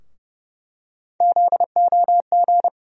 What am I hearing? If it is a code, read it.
Morse code, heard as a keyed tone, about 30 wpm, 700 Hz. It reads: ZOG
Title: USA Call Sign Postfixes - 3 Letters 30wpm